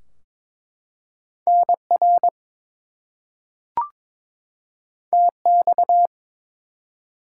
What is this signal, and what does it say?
Morse code, heard as a keyed tone, about 22 wpm, 700 Hz. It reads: NR TX
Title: Top 90 QSO Elements 22wpm